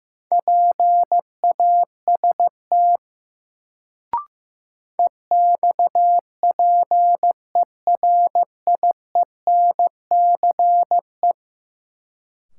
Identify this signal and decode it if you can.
Morse code, heard as a keyed tone, about 15 wpm, 700 Hz. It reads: PAST EXPERIENCE
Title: Sentences from Top 600 Words - New Word Drill - 15wpm